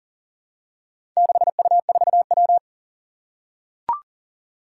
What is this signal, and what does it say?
Morse code, heard as a keyed tone, about 40 wpm, 700 Hz. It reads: 6U4W
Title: Four Challenging Letter-Number Combinations 40wpm